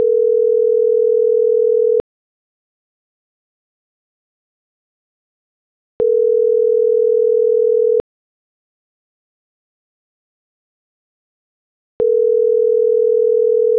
Ringback tone (North America), 440 + 480 Hz.